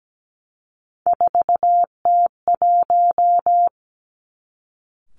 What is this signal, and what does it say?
Morse code, heard as a keyed tone, about 17 wpm, 700 Hz. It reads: 4T1
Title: Three Challenging Letter-Number Combinations 17wpm